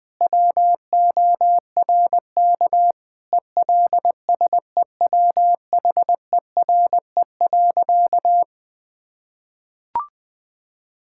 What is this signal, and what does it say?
Morse code, heard as a keyed tone, about 20 wpm, 700 Hz. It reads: WORK ELSEWHERE.
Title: Sentences from Top 1600 Words 20wpm